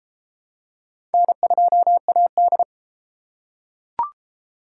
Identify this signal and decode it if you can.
Morse code, heard as a keyed tone, about 31 wpm, 700 Hz. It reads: N2AD